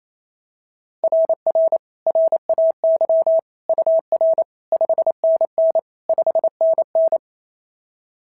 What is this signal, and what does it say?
Morse code, heard as a keyed tone, about 29 wpm, 690 Hz. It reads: RR RAY UR 5NN 5NN
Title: SOTA QSO Practice 28wpm